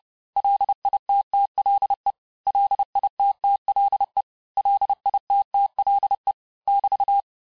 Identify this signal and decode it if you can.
Morse code, heard as a keyed tone, about 30 wpm, 800 Hz. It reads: LITTLE LITTLE LITTLE <BT>
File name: cours-du-mercredi / audios / cours-854-mots-vitesse-28-mpm